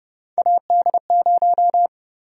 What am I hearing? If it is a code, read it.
Morse code, heard as a keyed tone, about 32 wpm, 700 Hz. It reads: AD0
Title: USA Call Sign Prefixes - 30wpm